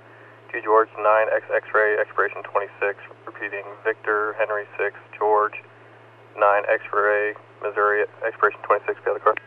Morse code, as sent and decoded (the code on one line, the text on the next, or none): none